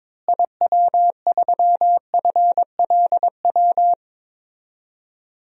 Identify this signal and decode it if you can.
Morse code, heard as a keyed tone, about 22 wpm, 700 Hz. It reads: IW3FLW
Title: DXCC-100 Call Signs - 6 Characters - Repeated 3x - 22wpm